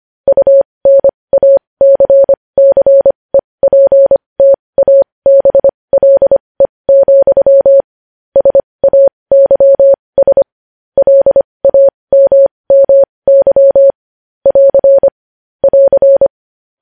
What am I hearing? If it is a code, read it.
Morse code, heard as a keyed tone, about 25 wpm, spaced as for 20 wpm, 550 Hz. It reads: UNACCEPTABLE, SAYS LAMMY <AR> <AR>
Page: News Headlines in Morse Code at 20 WPM